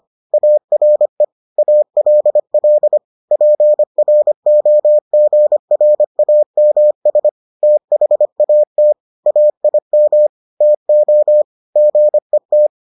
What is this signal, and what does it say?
Morse code, heard as a keyed tone, about 25 wpm, 600 Hz. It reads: ARE ALL PROGRAMS THAT AIM TO GET